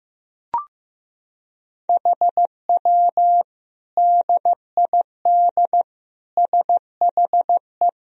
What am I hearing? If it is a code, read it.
Morse code, heard as a keyed tone, about 15 wpm, 700 Hz. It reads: HW DID SHE
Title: Sentences from Top 100 Words and Abbreviations 15wpm